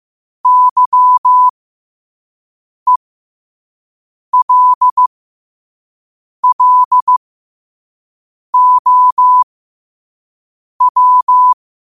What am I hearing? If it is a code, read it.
Morse code, heard as a keyed tone, about 15 wpm, spaced as for 5 wpm, 1000 Hz. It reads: YELLOW
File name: qotdfiles / 5wpm